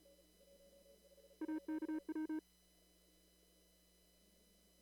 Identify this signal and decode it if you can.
Morse code, heard as a keyed tone, about 36 wpm, 310 Hz. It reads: AKW
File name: CW / kt5j wpx cw-op-k5pi